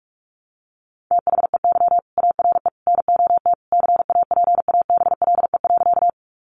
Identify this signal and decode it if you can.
Morse code, heard as a keyed tone, about 44 wpm, 700 Hz. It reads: THEY ARE NOT CAPABLE.
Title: Sentences from Top 1200 Words 45wpm